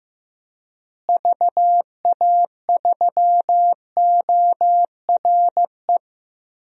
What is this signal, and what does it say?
Morse code, heard as a keyed tone, about 15 wpm, 700 Hz. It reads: VA3ORE
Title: DXCC-100 Call Signs - 6 Characters - Repeated 2x - 15wpm